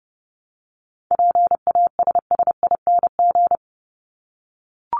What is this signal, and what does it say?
Morse code, heard as a keyed tone, about 30 wpm, 700 Hz. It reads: PASSING
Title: ING Suffix - Top 100 Words with Base Words 30wpm